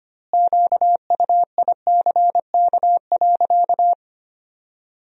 Morse code, heard as a keyed tone, about 25 wpm, 700 Hz. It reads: QUICK.
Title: 2x Word-Spacing - Sentences from Top 500 Words - 25wpm